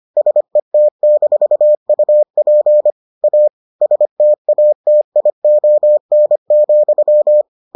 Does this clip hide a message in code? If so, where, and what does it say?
Yes: Morse code from 0.2 to 7.4 s, heard as a keyed tone, about 25 wpm, 600 Hz. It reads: SET-UP A STATION,